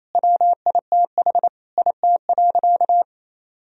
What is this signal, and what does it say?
Morse code, heard as a keyed tone, about 28 wpm, 700 Hz. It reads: WITH IT.
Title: Sentences from Top 1000 Words - Review Part 2 - 28wpm